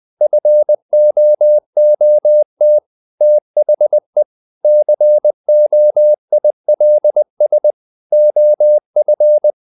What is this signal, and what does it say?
Morse code, heard as a keyed tone, about 20 wpm, 600 Hz. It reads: FOOT THE COILS OF
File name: the old man and the sea part 2 20wpm